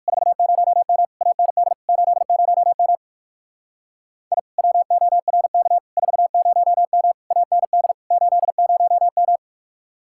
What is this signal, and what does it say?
Morse code, heard as a keyed tone, about 58 wpm, 700 Hz. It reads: 40M AND 80M I WORK 40M AND 80M